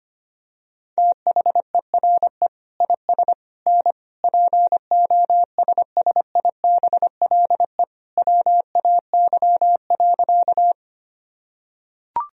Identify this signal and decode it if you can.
Morse code, heard as a keyed tone, about 25 wpm, 700 Hz. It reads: THERE IS N POSSIBLE WAY.
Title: Sentences from Top 500 Words and Abbreviations 25wpm